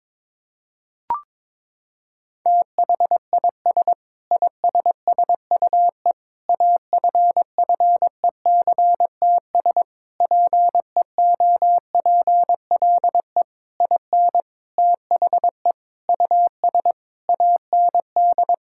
Morse code, heard as a keyed tone, about 21 wpm, 700 Hz. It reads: THIS ISSUE AFFECTS PEOPLE IN THE US AND
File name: Sentences with 7-Word Phrases - 22wpm